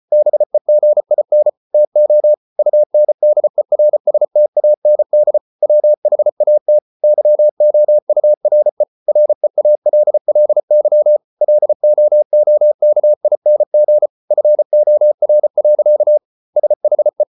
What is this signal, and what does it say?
Morse code, heard as a keyed tone, about 34 wpm, 600 Hz. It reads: BEGIN TO UNDERSTAND WHAT YOURE REALLY LOOKING FOR. SHE